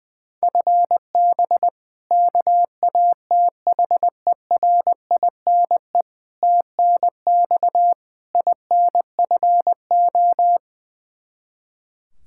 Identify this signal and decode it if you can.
Morse code, heard as a keyed tone, about 20 wpm, 700 Hz. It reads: FB KATHERINE TNX INFO
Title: K1USN SST QSO Practice 20wpm